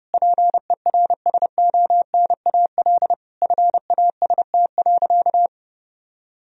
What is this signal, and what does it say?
Morse code, heard as a keyed tone, about 30 wpm, 700 Hz. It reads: PERSONAL FAST.
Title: Sentences from Top 1000 Words - Review Part 4 - 30wpm